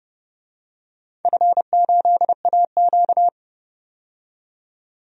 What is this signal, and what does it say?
Morse code, heard as a keyed tone, about 30 wpm, 700 Hz. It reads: F8AQ
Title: DXCC-100 Call Signs - 4 Characters - 30wpm